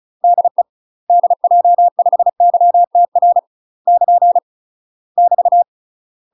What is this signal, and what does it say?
Morse code, heard as a keyed tone, about 34 wpm, 700 Hz. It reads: DE DJ5YTR <KN> <BT>
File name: UK MASTER QSOs TRAINING / 35WPM / UK MASTER QSOs 0038